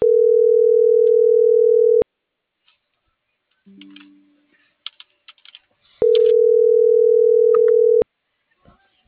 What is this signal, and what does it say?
Ringback tone (North America), 440 + 480 Hz.